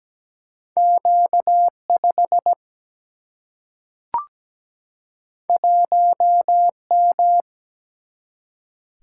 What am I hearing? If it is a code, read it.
Morse code, heard as a keyed tone, about 17 wpm, 700 Hz. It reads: Q5 1M
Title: Single-Letter Single-Number 17wpm